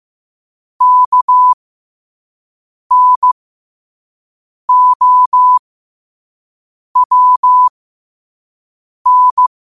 Morse code, heard as a keyed tone, about 15 wpm, spaced as for 5 wpm, 1000 Hz. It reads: KNOWN
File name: 5wpm